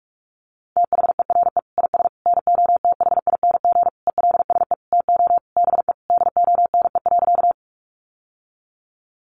Morse code, heard as a keyed tone, about 44 wpm, 700 Hz. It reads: THERE IS NOTHING ELSE TO BE DONE.
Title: Sentences from Top 800 Words 45wpm